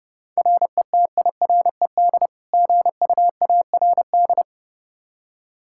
Morse code, heard as a keyed tone, about 30 wpm, 700 Hz. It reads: RETIRED GUARD
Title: Practice QSO - Exchange 3 - More Info - 30wpm